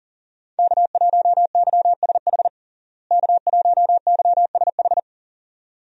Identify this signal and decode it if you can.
Morse code, heard as a keyed tone, about 40 wpm, 700 Hz. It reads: K1YSH K1YSH
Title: USA Call Signs - 5 Characters - Repeated 2x - 40wpm